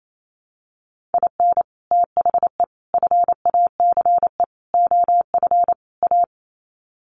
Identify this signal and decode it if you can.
Morse code, heard as a keyed tone, about 28 wpm, 700 Hz. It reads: IN THE FACE OF A
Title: Sets of 5 Words Encore 28wpm